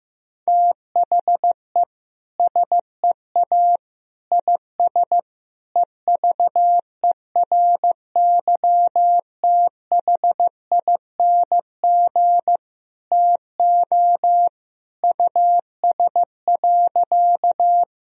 Morse code, heard as a keyed tone, about 15 wpm, 700 Hz. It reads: THE SEA IS EVERYTHING TO US.
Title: Sentences from Top 1000 Words - Review Part 1 - 15wpm